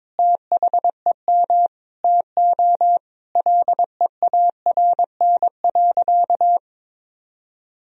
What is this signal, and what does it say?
Morse code, heard as a keyed tone, about 21 wpm, 700 Hz. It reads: THEM TO LEARN.
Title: Sentences from Top 1000 Words - Review Part 3 - 22wpm